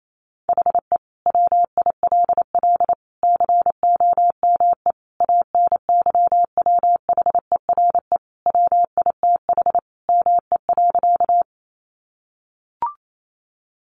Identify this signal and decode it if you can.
Morse code, heard as a keyed tone, about 28 wpm, 700 Hz. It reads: HE WILL COME ANYWHERE WITH ME.
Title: Sentences from Top 1400 Words 28wpm